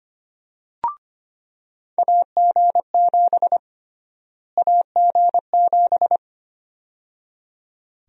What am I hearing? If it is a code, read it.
Morse code, heard as a keyed tone, about 25 wpm, 700 Hz. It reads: AG7 AG7
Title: USA Call Sign Prefixes - Repeated 2x - 25wpm